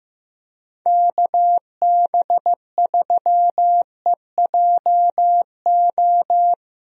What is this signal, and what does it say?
Morse code, heard as a keyed tone, about 16 wpm, 700 Hz. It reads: KB3EJO
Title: USA Call Signs - 6 Characters - Repeated 3x - 15wpm